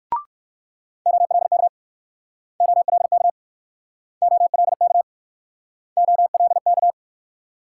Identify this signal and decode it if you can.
Morse code, heard as a keyed tone, about 58 wpm, 700 Hz. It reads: OLK OLK OLK OLK